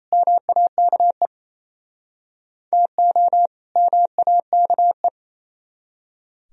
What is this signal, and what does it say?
Morse code, heard as a keyed tone, about 37 wpm, 700 Hz. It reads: MAKE TO MAKE